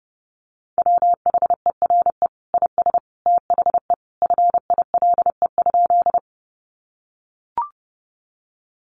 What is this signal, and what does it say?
Morse code, heard as a keyed tone, about 30 wpm, 700 Hz. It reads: WHERE IS THE FILE?
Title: Sentences from Top 1200 Words 30wpm